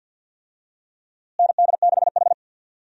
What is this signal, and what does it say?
Morse code, heard as a keyed tone, about 52 wpm, 700 Hz. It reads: ND6H
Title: USA Call Signs - 4 Characters - Repeated 2x - 50wpm